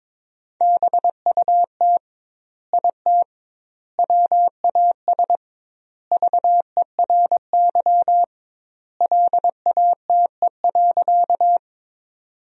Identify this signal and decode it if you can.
Morse code, heard as a keyed tone, about 22 wpm, 700 Hz. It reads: BUT IT WAS VERY LATE.
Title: 2x Word-Spacing - Sentences from Top 300 Words - 22wpm